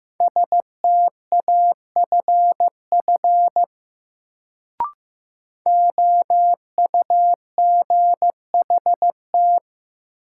Morse code, heard as a keyed tone, about 15 wpm, 700 Hz. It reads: STAFF OUGHT
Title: Sets of 5 Words Encore - New Word Drill - 15wpm